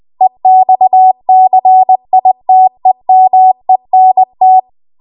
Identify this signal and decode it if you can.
Morse code, heard as a keyed tone, about 20 wpm, 750 Hz. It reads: EXCITEMENT
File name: Archive / 20 WPM / 20WPM